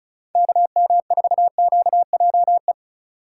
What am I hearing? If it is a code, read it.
Morse code, heard as a keyed tone, about 36 wpm, 700 Hz. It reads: KM4QJE
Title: USA Call Signs - 3x Character Spacing - 35wpm